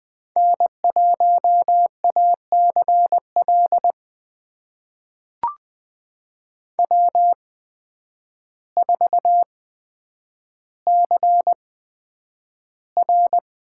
Morse code, heard as a keyed tone, about 20 wpm, spaced as for 9 wpm, 700 Hz. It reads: N1ACL W4CR
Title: USA Call Signs - 8x Character Spacing - 20wpm